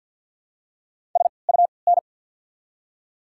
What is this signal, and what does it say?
Morse code, heard as a keyed tone, about 48 wpm, 690 Hz. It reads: S U N
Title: 3x Character Spacing - Top 300 Words - 50wpm